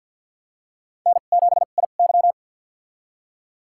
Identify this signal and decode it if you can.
Morse code, heard as a keyed tone, about 52 wpm, spaced as for 37 wpm, 690 Hz. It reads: N7IY
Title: USA Call Signs - 2x Character Spacing - 50wpm